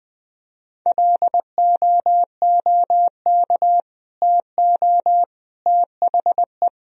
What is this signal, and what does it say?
Morse code, heard as a keyed tone, about 21 wpm, 700 Hz. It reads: LOOK TO THE
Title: Sentences from Top 1000 Words - Review Part 3 - 20wpm